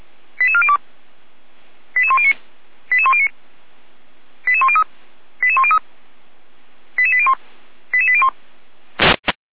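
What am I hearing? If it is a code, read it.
Five-tone selective call (ZVEI), heard as quick runs of five tones, tones of about 70 ms. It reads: 88402 88109 88109 88103 88103 88801 88801
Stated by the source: ZVEI five-tone call received on SDR and logged by Feuerwehr Orlamünde, a fire brigade.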